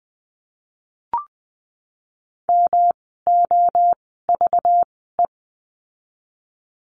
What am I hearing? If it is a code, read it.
Morse code, heard as a keyed tone, about 20 wpm, spaced as for 14 wpm, 700 Hz. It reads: MOVE
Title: 244 Lesson 39f - Letter z - Words 20wpm